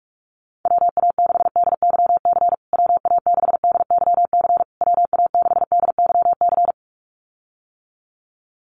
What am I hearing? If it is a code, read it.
Morse code, heard as a keyed tone, about 44 wpm, 690 Hz. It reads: WA6DYC WA6DYC WA6DYC
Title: USA Call Signs Repeated 3x 45wpm